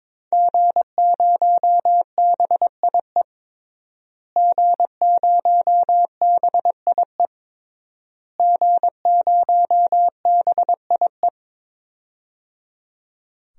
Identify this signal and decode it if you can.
Morse code, heard as a keyed tone, about 22 wpm, 700 Hz. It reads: G0BIE G0BIE G0BIE
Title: DXCC-100 Call Signs - 5 Characters - Repeated 3x - 22wpm